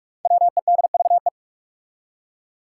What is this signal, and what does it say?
Morse code, heard as a keyed tone, about 46 wpm, 700 Hz. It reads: WEDVE
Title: Five Challenging Letter-Number Combinations 45wpm